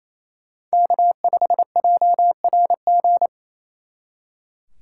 Morse code, heard as a keyed tone, about 36 wpm, 700 Hz. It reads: K5JRG